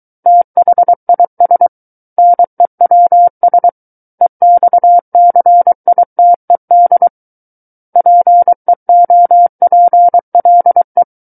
Morse code, heard as a keyed tone, about 23 wpm, 700 Hz. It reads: THIS NEWS EXCITED PEOPLE
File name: Book One / 14-Chapter 14 - 23 WPM